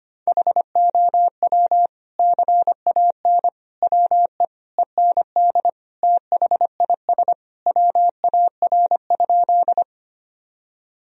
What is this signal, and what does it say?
Morse code, heard as a keyed tone, about 25 wpm, 700 Hz. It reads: HOW CAN WE END THIS WAR?
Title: Sentences from Top 1000 Words - Review Part 2 - 25wpm